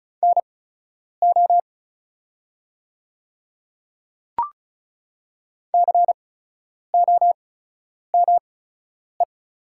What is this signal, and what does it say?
Morse code, heard as a keyed tone, about 36 wpm, spaced as for 10 wpm, 700 Hz. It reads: NO COME
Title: 8x Character Spacing - Top 300 Words - 35wpm